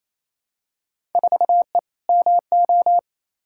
Morse code, heard as a keyed tone, about 28 wpm, 700 Hz. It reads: 4E MO
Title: Field Day Exchanges Encore - 28wpm